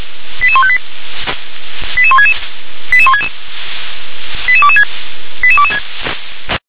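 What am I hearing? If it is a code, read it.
Five-tone selective call (ZVEI), heard as quick runs of five tones, tones of about 70 ms. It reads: 88158 88166 88166 88226 88226
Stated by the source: Five-tone ZVEI selective call received on an SDR.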